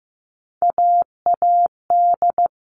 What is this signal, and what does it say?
Morse code, heard as a keyed tone, about 15 wpm, 700 Hz. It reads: AAD
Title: All Cut Numbers - 3-Digit 15wpm